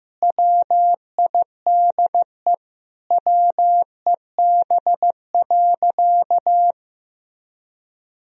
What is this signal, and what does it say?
Morse code, heard as a keyed tone, about 15 wpm, 700 Hz. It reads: WIDE WEB.
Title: Sentences from Top 1000 Words - Review Part 4 - 15wpm